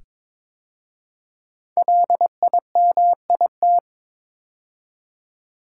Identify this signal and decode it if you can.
Morse code, heard as a keyed tone, about 23 wpm, 700 Hz. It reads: LIMIT
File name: 5 Letter Words - First 3 Letters Given - 22wpm